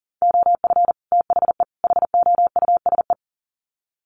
Morse code, heard as a keyed tone, about 40 wpm, 700 Hz. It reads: OF THE HOUSE
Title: Sets of 3 Words 40wpm